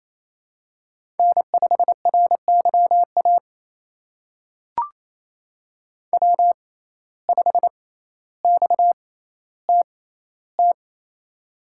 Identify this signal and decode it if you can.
Morse code, heard as a keyed tone, about 27 wpm, spaced as for 14 wpm, 700 Hz. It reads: N5RYA W5XTT
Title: USA Call Signs - 6x Character Spacing - 28wpm